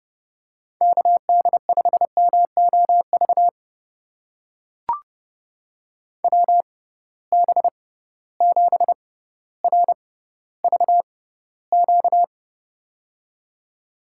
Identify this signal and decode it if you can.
Morse code, heard as a keyed tone, about 30 wpm, spaced as for 15 wpm, 700 Hz. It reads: KD5MOV WB7RVQ
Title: USA Call Signs - 6 Characters - 6x Character Spacing - 30wpm